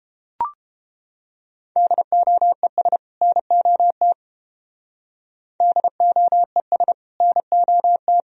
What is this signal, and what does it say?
Morse code, heard as a keyed tone, about 30 wpm, 700 Hz. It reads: DOES NOT DOES NOT